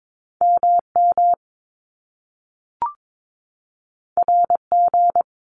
Morse code, heard as a keyed tone, about 22 wpm, 700 Hz. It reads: MM RG
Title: Top Two Letter Combinations in Words 22wpm